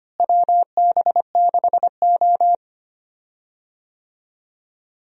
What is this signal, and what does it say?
Morse code, heard as a keyed tone, about 25 wpm, 700 Hz. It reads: WB6O
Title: USA Call Signs - 4 Characters - Repeated 2x - 25wpm